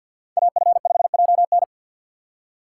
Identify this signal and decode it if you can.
Morse code, heard as a keyed tone, about 53 wpm, 700 Hz. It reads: AF5JN
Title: USA Call Signs - 5 Characters - 2x Character Spacing - 50wpm